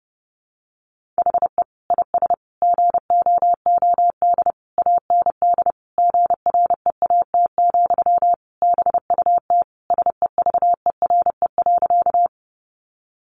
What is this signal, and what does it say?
Morse code, heard as a keyed tone, about 30 wpm, 700 Hz. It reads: HE IS GOOD AND GREAT, BUT SEVERE.
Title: Sentences from Top 1400 Words 30wpm